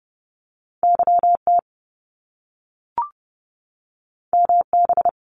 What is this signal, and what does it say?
Morse code, heard as a keyed tone, about 30 wpm, 700 Hz. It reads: YT MB
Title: Canadian Provinces and Territories 30wpm